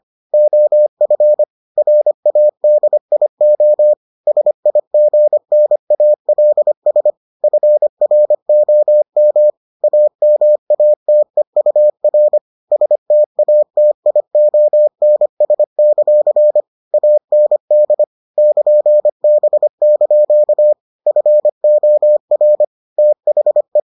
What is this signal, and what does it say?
Morse code, heard as a keyed tone, about 25 wpm, 600 Hz. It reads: OF RADIO SIGNALS FROM AMATEUR STATIONS; AND <KN>B) FOR THE